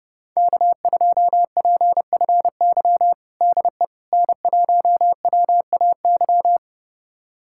Morse code, heard as a keyed tone, about 32 wpm, 700 Hz. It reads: K2PFY DE N1WAY
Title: USA Callsign DE Callsign 30wpm